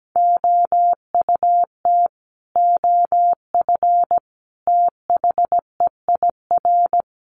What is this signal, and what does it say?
Morse code, heard as a keyed tone, about 17 wpm, 700 Hz. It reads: OUT OF THEIR
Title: Sets of 5 Words Encore2 17wpm